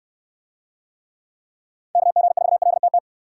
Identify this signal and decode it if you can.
Morse code, heard as a keyed tone, about 67 wpm, 700 Hz. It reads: KG4LTT